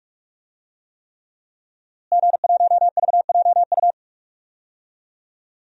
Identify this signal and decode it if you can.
Morse code, heard as a keyed tone, about 43 wpm, 700 Hz. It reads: G1VJU
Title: DXCC-100 Call Signs - 5 Characters - Repeated 3x - 45wpm